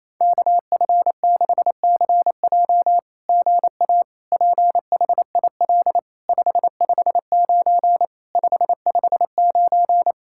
Morse code, heard as a keyed tone, about 27 wpm, 700 Hz. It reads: KF6CJ GA PHIL 559 559
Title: SOTA QSO Practice 28wpm